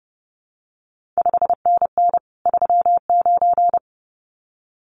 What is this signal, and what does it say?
Morse code, heard as a keyed tone, about 32 wpm, 700 Hz. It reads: 5NN 39
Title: CQ WW DX Contest Exchange Encore - 30wpm